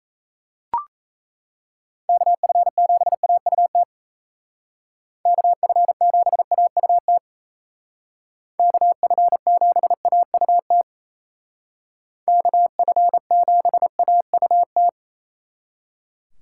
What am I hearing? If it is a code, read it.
Morse code, heard as a keyed tone, about 34 wpm, 700 Hz. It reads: KF7AUT KF7AUT KF7AUT KF7AUT
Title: Speed Racing - USA 2x3 Call Signs 28wpm